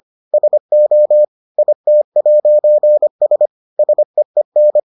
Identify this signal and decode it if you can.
Morse code, heard as a keyed tone, about 25 wpm, 600 Hz. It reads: SO IT'S SEEN